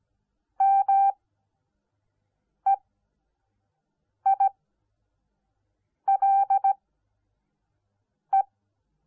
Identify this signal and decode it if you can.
Morse code, heard as a keyed tone, about 17 wpm, spaced as for 5 wpm, 780 Hz. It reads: MEILE